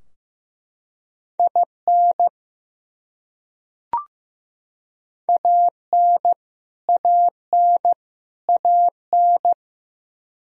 Morse code, heard as a keyed tone, about 15 wpm, 700 Hz. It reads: IN AN AN AN
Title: Top 100 Words Repeated 15wpm